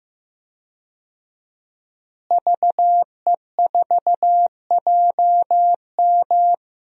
Morse code, heard as a keyed tone, about 16 wpm, 700 Hz. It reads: VE4JM
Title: DXCC-100 Call Signs - 5 Characters - Repeated 3x - 15wpm